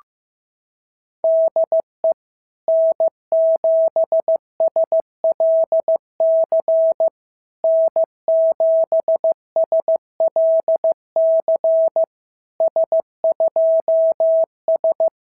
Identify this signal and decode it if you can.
Morse code, heard as a keyed tone, about 15 wpm, 660 Hz. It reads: DE N7SLC N7SLC S2S
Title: SOTA QSO Practice 15wpm